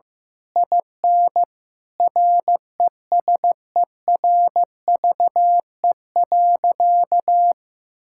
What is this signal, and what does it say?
Morse code, heard as a keyed tone, about 15 wpm, 700 Hz. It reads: IN RESERVE.
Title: Sentences from Top 1000 Words - Review Part 6 - 15wpm